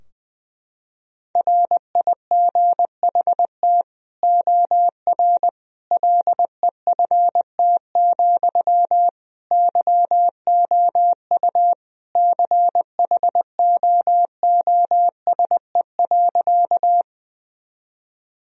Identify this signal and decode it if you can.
Morse code, heard as a keyed tone, about 20 wpm, 700 Hz. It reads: RIGHT OR LEFT, YOU CHOOSE.
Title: Sentences from Top 1000 Words - Review Part 4 - 20wpm